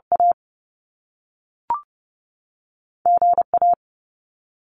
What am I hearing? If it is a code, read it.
Morse code, heard as a keyed tone, about 30 wpm, spaced as for 3 wpm, 700 Hz. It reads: AG A
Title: Top Two Letter Combinations in Words 30wpm